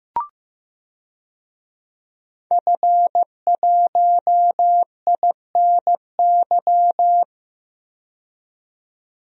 Morse code, heard as a keyed tone, about 15 wpm, 700 Hz. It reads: F1INY
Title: DXCC-100 Call Signs - 5 Characters - 15wpm